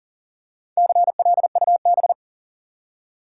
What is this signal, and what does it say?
Morse code, heard as a keyed tone, about 40 wpm, 700 Hz. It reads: CLUB
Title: Four Letter Words - 40wpm